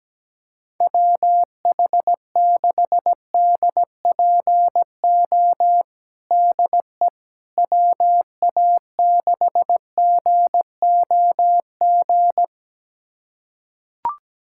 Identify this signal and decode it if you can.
Morse code, heard as a keyed tone, about 17 wpm, 700 Hz. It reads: WH6DPO DE WA6GOG
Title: USA Callsign DE Callsign 17wpm